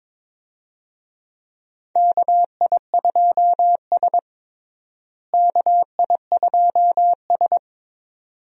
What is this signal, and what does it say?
Morse code, heard as a keyed tone, about 21 wpm, 700 Hz. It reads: KI2S KI2S
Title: USA Call Signs - 4 Characters - Repeated 2x - 22wpm